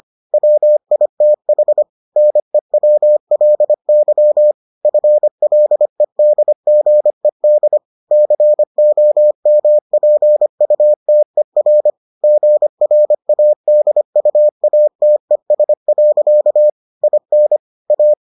Morse code, heard as a keyed tone, about 25 wpm, 600 Hz. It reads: WITH NEWLY FLEDGED COMPUTER GRADUATES. IN A